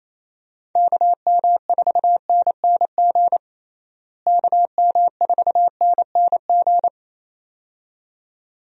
Morse code, heard as a keyed tone, about 27 wpm, 700 Hz. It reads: KM4NNG KM4NNG
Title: USA Call Signs - 6 Characters - Repeated 2x - 28wpm